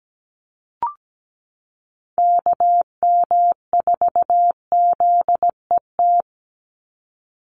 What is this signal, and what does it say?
Morse code, heard as a keyed tone, about 17 wpm, 700 Hz. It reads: KM4ZET
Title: USA Call Signs 17wpm